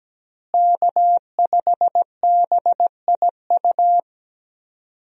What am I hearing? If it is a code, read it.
Morse code, heard as a keyed tone, about 17 wpm, 700 Hz. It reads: K5BIU
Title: USA Call Signs - 5 Characters - Repeated 2x - 17wpm